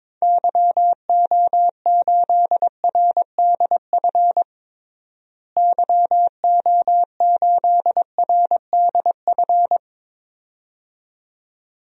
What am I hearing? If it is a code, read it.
Morse code, heard as a keyed tone, about 21 wpm, 700 Hz. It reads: YO8RDF YO8RDF
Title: DXCC-100 Call Signs - 6 Characters - Repeated 2x - 22wpm